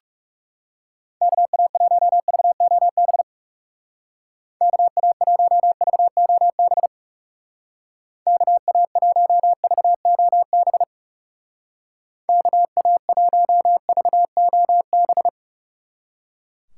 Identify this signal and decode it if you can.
Morse code, heard as a keyed tone, about 36 wpm, 700 Hz. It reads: KA1VOB KA1VOB KA1VOB KA1VOB
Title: Speed Racing - USA 2x3 Call Signs 30wpm